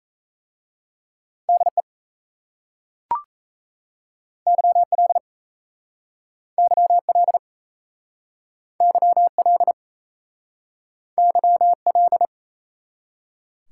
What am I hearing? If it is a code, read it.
Morse code, heard as a keyed tone, about 36 wpm, 700 Hz. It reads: DE YL YL YL YL